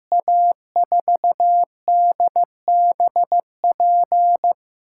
Morse code, heard as a keyed tone, about 15 wpm, 700 Hz. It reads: A4DBP
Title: USA Call Signs - 6 Characters - 6x Character Spacing - 15wpm